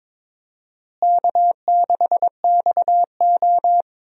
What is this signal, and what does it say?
Morse code, heard as a keyed tone, about 22 wpm, 700 Hz. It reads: K6XO
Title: USA Call Signs - 4 Characters - Repeated 3x - 22wpm